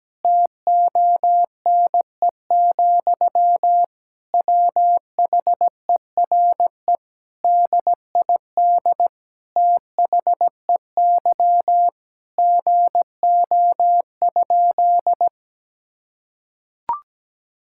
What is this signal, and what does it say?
Morse code, heard as a keyed tone, about 17 wpm, 700 Hz. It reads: TONE, WHERE DID THEY GO?
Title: Sentences from Top 1600 Words 17wpm